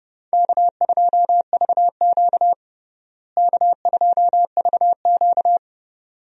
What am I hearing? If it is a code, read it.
Morse code, heard as a keyed tone, about 32 wpm, 700 Hz. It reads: K2VQ K2VQ
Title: USA Call Signs - 4 Characters - Repeated 2x - 30wpm